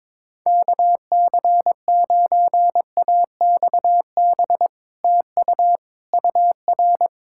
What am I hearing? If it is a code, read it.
Morse code, heard as a keyed tone, about 22 wpm, 700 Hz. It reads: KC9AXB TU UR
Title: Speed-Racing Common QSO Phrases Encore - 22wpm